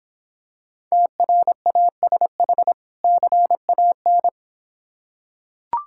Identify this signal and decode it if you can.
Morse code, heard as a keyed tone, about 26 wpm, 700 Hz. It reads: TRASH CAN